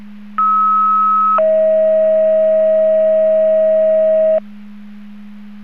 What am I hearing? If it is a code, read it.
Two-tone page, heard as a one-second tone, then a three-second tone, about 1290 then 640 Hz. It is heard from 0.4 to 4.4 s.
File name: Portage 154.385 Paging